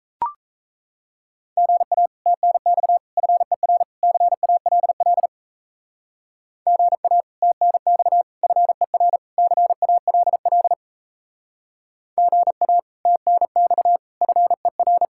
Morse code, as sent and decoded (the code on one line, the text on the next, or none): --. .- / - -. -..- / ..-. .-. / -.-. .- .-.. .-.. / --. .- / - -. -..- / ..-. .-. / -.-. .- .-.. .-.. / --. .- / - -. -..- / ..-. .-.
GA TNX FER CALL GA TNX FER CALL GA TNX FER